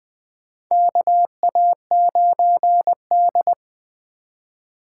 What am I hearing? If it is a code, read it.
Morse code, heard as a keyed tone, about 20 wpm, 700 Hz. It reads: KA9D
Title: USA Call Signs - 4 Characters - 4x Character Spacing - 20wpm